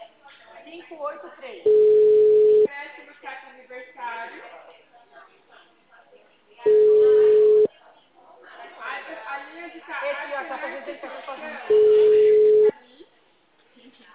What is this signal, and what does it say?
Ringback tone (Europe, 425 Hz), with speech.